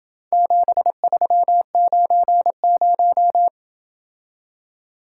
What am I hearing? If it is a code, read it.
Morse code, heard as a keyed tone, about 26 wpm, 700 Hz. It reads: 7390